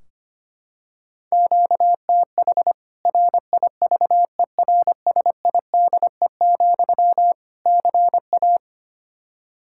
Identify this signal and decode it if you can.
Morse code, heard as a keyed tone, about 25 wpm, 700 Hz. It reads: QTH RIVERSIDE, CA
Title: Practice QSO - Exchange 1 - Beginning - 25wpm